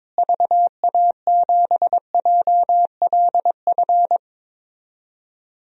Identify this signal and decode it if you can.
Morse code, heard as a keyed tone, about 22 wpm, 700 Hz. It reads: VA7JLF
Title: DXCC-100 Call Signs - 6 Characters - Repeated 2x - 22wpm